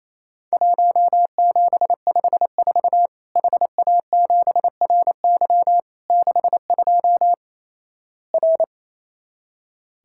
Morse code, heard as a keyed tone, about 27 wpm, 700 Hz. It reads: 1754 HA7RY 62 R
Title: Worked All Europe DX Contest - QTC Exchange - 28wpm